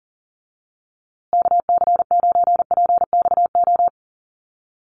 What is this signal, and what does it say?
Morse code, heard as a keyed tone, about 37 wpm, 700 Hz. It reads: KC9PXO
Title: USA Call Signs Repeated 3x 40wpm